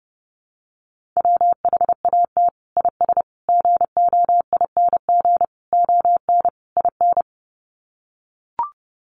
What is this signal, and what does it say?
Morse code, heard as a keyed tone, about 28 wpm, 700 Hz. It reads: WHAT IS GOING ON IN
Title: Sets of 5 Words Encore2 30wpm